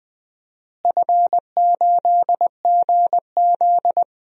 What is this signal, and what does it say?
Morse code, heard as a keyed tone, about 20 wpm, 700 Hz. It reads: F8GZ
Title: DXCC-100 Call Signs - 4 Characters - Repeated 2x - 20wpm